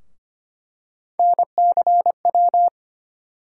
Morse code, heard as a keyed tone, about 25 wpm, 700 Hz. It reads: NCW
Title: USA Call Sign Postfixes - 3 Letters 25wpm